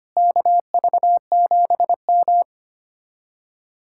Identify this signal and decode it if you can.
Morse code, heard as a keyed tone, about 32 wpm, 700 Hz. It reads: KV7M